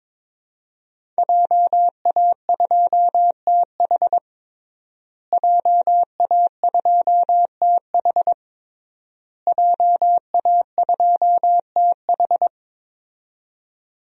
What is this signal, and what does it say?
Morse code, heard as a keyed tone, about 22 wpm, 700 Hz. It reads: JA2TH JA2TH JA2TH
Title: DXCC-100 Call Signs - 5 Characters - Repeated 3x - 22wpm